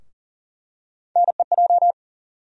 Morse code, heard as a keyed tone, about 40 wpm, 700 Hz. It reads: NEJ
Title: USA Call Sign Postfixes - 3 Letters 40wpm